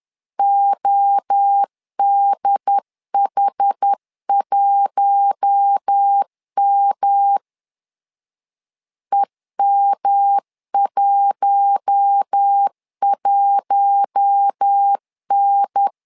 Morse code, heard as a keyed tone, about 10 wpm, 800 Hz. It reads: ODH1M EM11N
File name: LEZIONI / 4 Q1 10wpm 2448